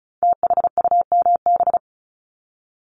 Morse code, heard as a keyed tone, about 36 wpm, 700 Hz. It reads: THUMB
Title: Multi-letter Phonemes Repeated 3x 35wpm